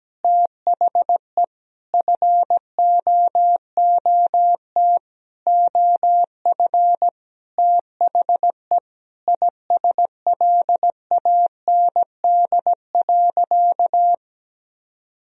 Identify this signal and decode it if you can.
Morse code, heard as a keyed tone, about 17 wpm, 700 Hz. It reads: THE FOOT OF THE ISLAND.
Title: Sentences from Top 1000 Words - Review Part 3 - 17wpm